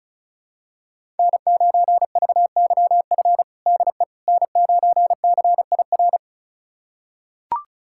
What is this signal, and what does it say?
Morse code, heard as a keyed tone, about 34 wpm, 700 Hz. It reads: N9VYF DE N9CIR
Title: USA Callsign DE Callsign - 5 Characters - 35wpm